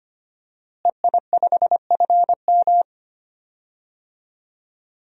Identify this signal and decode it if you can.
Morse code, heard as a keyed tone, about 25 wpm, 700 Hz. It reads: EI5FM
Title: DXCC-100 Call Signs - 5 Characters - Repeated 2x - 25wpm